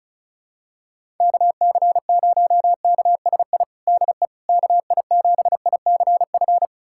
Morse code, heard as a keyed tone, about 34 wpm, 700 Hz. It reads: KC0KSI DE KI7ICF
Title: USA Callsign DE Callsign - 6 Characters - 35wpm